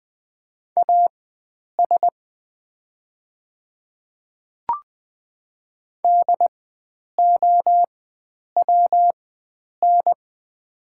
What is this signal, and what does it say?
Morse code, heard as a keyed tone, about 20 wpm, spaced as for 9 wpm, 690 Hz. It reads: AS DOWN
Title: 4x Character Spacing - Top 100 Words - 20wpm